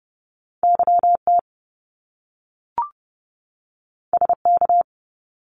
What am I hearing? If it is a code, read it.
Morse code, heard as a keyed tone, about 30 wpm, 700 Hz. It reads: YT SK
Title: Canadian Provinces and Territories 30wpm